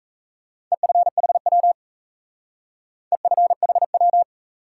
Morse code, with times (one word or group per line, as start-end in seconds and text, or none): 0.7-1.7 EFHW
3.1-4.2 EFHW